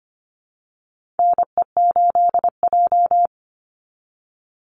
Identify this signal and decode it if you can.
Morse code, heard as a keyed tone, about 25 wpm, 700 Hz. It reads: NE8J
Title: Call Signs Repeated 3x - 25wpm